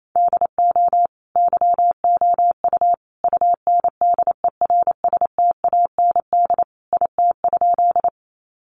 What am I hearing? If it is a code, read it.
Morse code, heard as a keyed tone, about 28 wpm, 700 Hz. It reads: DO YOU UNDERSTAND IT?
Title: Sentences from Top 600 Words 28wpm